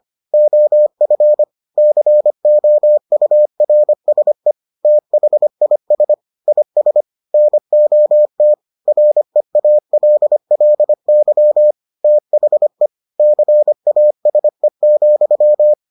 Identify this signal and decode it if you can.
Morse code, heard as a keyed tone, about 25 wpm, 600 Hz. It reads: OF COURSE THIS IS NOT REALLY THE CASE,